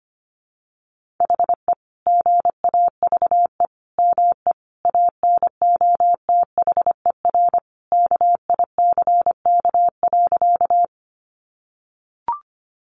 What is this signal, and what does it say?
Morse code, heard as a keyed tone, about 25 wpm, 700 Hz. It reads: HE GAVE ME ANOTHER KICK.
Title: Sentences from Top 1300 Words 25wpm